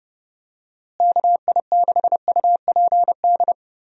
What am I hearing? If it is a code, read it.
Morse code, heard as a keyed tone, about 32 wpm, 700 Hz. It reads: KI6UPD
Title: USA Call Signs - 6 Characters - Repeated 3x - 30wpm